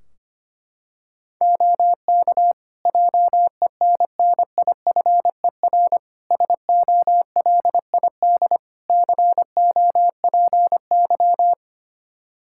Morse code, heard as a keyed tone, about 25 wpm, 700 Hz. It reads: OK JENNIFER SOLID COPY
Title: Practice QSO - Exchange 3 - More Info - 25wpm